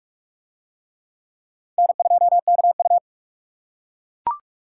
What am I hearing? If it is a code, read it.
Morse code, heard as a keyed tone, about 39 wpm, 700 Hz. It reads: N2KU